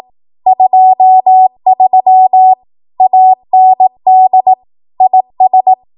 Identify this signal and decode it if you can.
Morse code, heard as a keyed tone, about 18 wpm, 750 Hz. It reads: 23 AND IS
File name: Morse / 18WPM